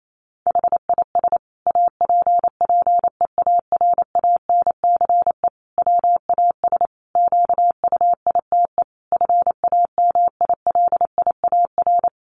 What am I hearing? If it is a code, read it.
Morse code, heard as a keyed tone, about 28 wpm, 700 Hz. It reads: HIS APPEARANCE WAS QUITE FAMILIAR
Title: Sentences from Top 1100 Words 28wpm